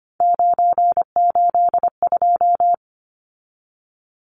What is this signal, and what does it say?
Morse code, heard as a keyed tone, about 25 wpm, 700 Hz. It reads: 982
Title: Numbers - 3-Digit 25wpm